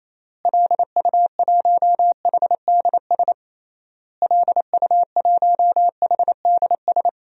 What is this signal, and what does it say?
Morse code, heard as a keyed tone, about 27 wpm, 700 Hz. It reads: LU1HDS LU1HDS
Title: DXCC-100 Call Signs - 6 Characters - Repeated 3x - 28wpm